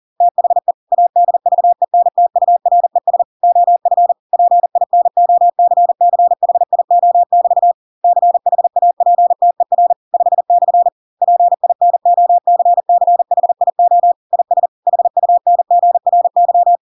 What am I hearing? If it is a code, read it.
Morse code, heard as a keyed tone, about 40 wpm, 700 Hz. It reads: THE ADVENTURES OF PINOCCHIO<BT> CHAPTER 5/ PINOCCHIO IS HUNGRY
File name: Pinocchio / Pinocchio ch5 40 40 ik1hge